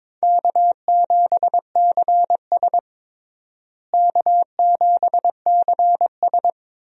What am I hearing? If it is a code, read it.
Morse code, heard as a keyed tone, about 22 wpm, 700 Hz. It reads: K7CS K7CS
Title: USA Call Signs - 4 Characters - Repeated 2x - 22wpm